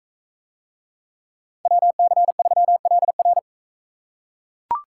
Morse code, heard as a keyed tone, about 41 wpm, 700 Hz. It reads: WC3LR